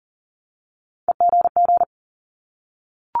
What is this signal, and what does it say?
Morse code, heard as a keyed tone, about 40 wpm, 700 Hz. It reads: EGG
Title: Multi-letter Phonemes 40wpm